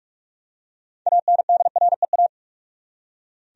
Morse code, heard as a keyed tone, about 43 wpm, 700 Hz. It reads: ANDREA
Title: USA Common Names - 45wpm